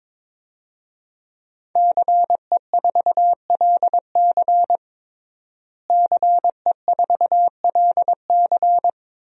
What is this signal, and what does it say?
Morse code, heard as a keyed tone, about 22 wpm, 700 Hz. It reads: CE4LC CE4LC
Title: DXCC-100 Call Signs - 5 Characters - Repeated 2x - 22wpm